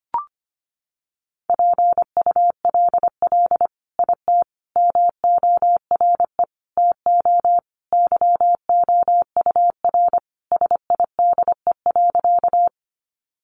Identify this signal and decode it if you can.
Morse code, heard as a keyed tone, about 25 wpm, 700 Hz. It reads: PULL IT MORE TO YOUR SIDE.
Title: Sentences from Top 400 Words 25wpm